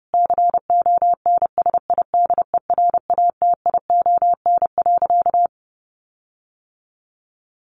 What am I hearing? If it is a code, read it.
Morse code, heard as a keyed tone, about 30 wpm, 700 Hz. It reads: CONSIDERATION.
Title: Sentences from Top 1500 Words 30wpm